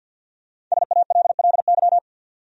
Morse code, heard as a keyed tone, about 51 wpm, 700 Hz. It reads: SALLY
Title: USA Common Names - Max 5 Letters - 50wpm